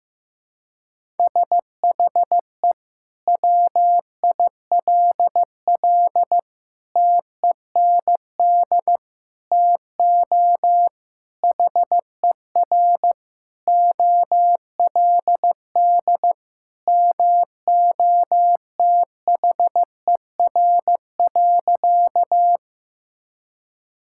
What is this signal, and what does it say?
Morse code, heard as a keyed tone, about 15 wpm, 700 Hz. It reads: SHE WILL TEND TO HER OLD MOTHER.
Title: Sentences from Top 1000 Words - Review Part 5 - 15wpm